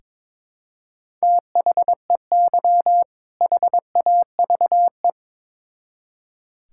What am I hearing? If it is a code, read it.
Morse code, heard as a keyed tone, about 21 wpm, 700 Hz. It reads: THEY HAVE
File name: Build A Phrase - 2-Word Phrases - 22wpm